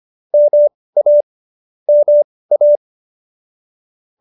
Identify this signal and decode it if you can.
Morse code, heard as a keyed tone, about 25 wpm, spaced as for 18 wpm, 590 Hz. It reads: MA MA